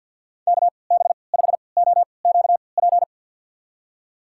Morse code, heard as a keyed tone, about 48 wpm, spaced as for 28 wpm, 700 Hz. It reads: KB5OQP
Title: USA Call Signs - 3x Character Spacing - 50wpm